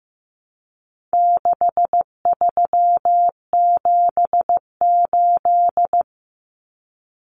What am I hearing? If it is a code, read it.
Morse code, heard as a keyed tone, about 15 wpm, 700 Hz. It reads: 6378
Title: Numbers - 4-Digit 15wpm